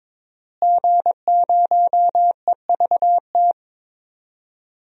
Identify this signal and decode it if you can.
Morse code, heard as a keyed tone, about 21 wpm, 700 Hz. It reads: G0EVT
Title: DXCC-100 Call Signs - 5 Characters - Repeated 3x - 22wpm